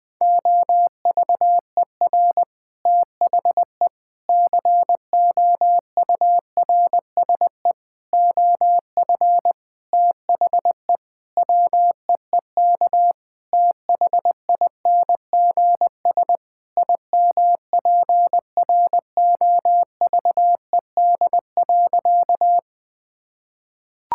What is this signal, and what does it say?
Morse code, heard as a keyed tone, about 20 wpm, 700 Hz. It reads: OVER THE COURSE OF THE WEEK THINGS IMPROVED.
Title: Sentences with 4-Word Phrases - 20wpm